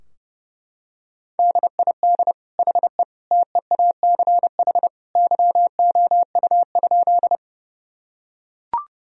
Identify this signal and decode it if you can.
Morse code, heard as a keyed tone, about 30 wpm, 700 Hz. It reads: DID HE TEACH YOU?
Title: Sentences from Top 1000 Words - Review Part 2 - 30wpm